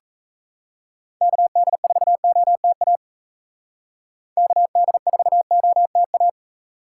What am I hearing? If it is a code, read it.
Morse code, heard as a keyed tone, about 36 wpm, 700 Hz. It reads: KD4OTA KD4OTA